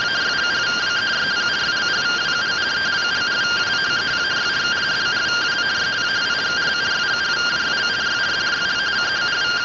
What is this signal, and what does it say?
Two-tone teleprinter signal, about 170 Hz shift.